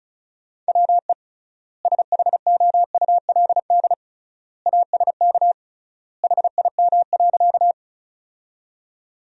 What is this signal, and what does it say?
Morse code, heard as a keyed tone, about 36 wpm, 700 Hz. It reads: WE SHOULD ASK HIM.
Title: 3x Word-Spacing - Sentences from Top 300 Words - 35wpm